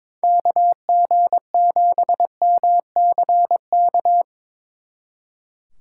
Morse code, heard as a keyed tone, about 24 wpm, 700 Hz. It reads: KG7MCK
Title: Speed Racing - USA 2x3 Call Signs 22wpm